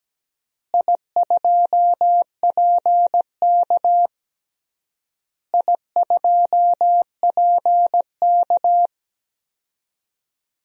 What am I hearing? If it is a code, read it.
Morse code, heard as a keyed tone, about 17 wpm, 700 Hz. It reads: I2PK I2PK
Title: DXCC-100 Call Signs - 4 Characters - Repeated 3x - 17wpm